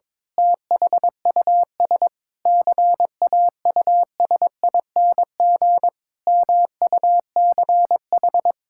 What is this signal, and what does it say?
Morse code, heard as a keyed tone, about 22 wpm, 700 Hz. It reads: THUS CAUSING MUCH
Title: Pangram Challenge v2 - 22wpm